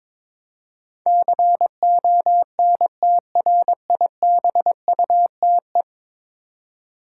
Morse code, heard as a keyed tone, about 22 wpm, 700 Hz. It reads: CONTRIBUTE
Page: morse-code-ninja-practice-22wpm